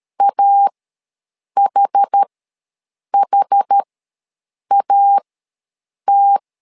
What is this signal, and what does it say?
Morse code, heard as a keyed tone, about 13 wpm, spaced as for 7 wpm, 800 Hz. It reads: AHHAT
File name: LEZIONI / 2 UHDN / 2 UH ws 12wpm 1730